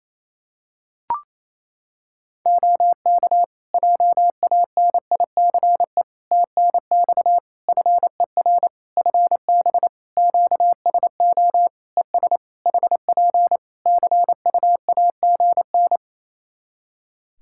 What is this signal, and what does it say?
Morse code, heard as a keyed tone, about 28 wpm, 700 Hz. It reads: OK JANICE TNX FER FB QSO ES HP CUAGN
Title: K1USN SST QSO Practice 28wpm